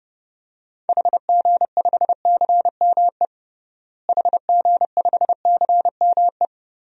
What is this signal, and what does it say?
Morse code, heard as a keyed tone, about 32 wpm, 700 Hz. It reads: HG5CME HG5CME
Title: DXCC-100 Call Signs - 6 Characters - Repeated 3x - 30wpm